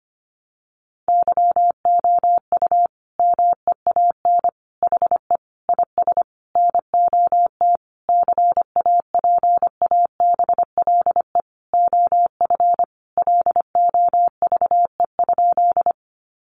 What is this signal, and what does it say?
Morse code, heard as a keyed tone, about 25 wpm, 700 Hz. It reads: YOU MEAN HE IS NOT CAPABLE OF LOVE?
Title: Sentences from Top 1200 Words 25wpm